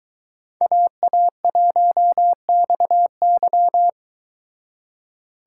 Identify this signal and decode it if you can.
Morse code, heard as a keyed tone, about 22 wpm, 700 Hz. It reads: AA1XY